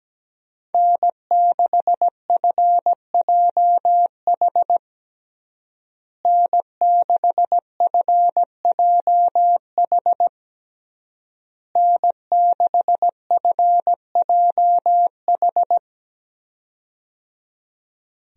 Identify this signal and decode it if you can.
Morse code, heard as a keyed tone, about 17 wpm, 700 Hz. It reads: N6FJH N6FJH N6FJH
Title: USA Call Signs - 5 Characters - Repeated 3x - 17wpm